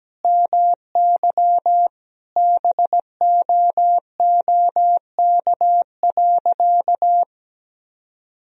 Morse code, heard as a keyed tone, about 17 wpm, 700 Hz. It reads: MY BOOK.
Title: Sentences from Top 1000 Words - Review Part 3 - 17wpm